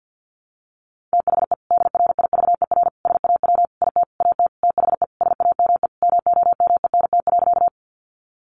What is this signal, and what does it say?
Morse code, heard as a keyed tone, about 50 wpm, 700 Hz. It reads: THE DRIVER SAW IT AT THE SAME MOMENT.
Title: Sentences from Top 1000 Words 50wpm